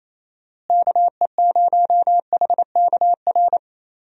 Morse code, heard as a keyed tone, about 28 wpm, 700 Hz. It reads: KE0HKR
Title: USA Call Signs - 6 Characters - Repeated 3x - 28wpm